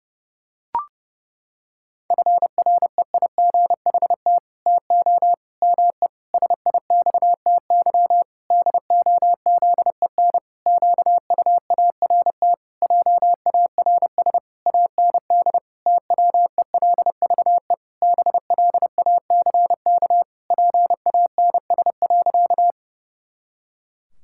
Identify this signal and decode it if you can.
Morse code, heard as a keyed tone, about 32 wpm, 700 Hz. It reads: FREIGHT TO ME SIXTY DOZEN QUART JARS AND TWELVE BLACK PANS.
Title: Pangram Challenge v2 - 30wpm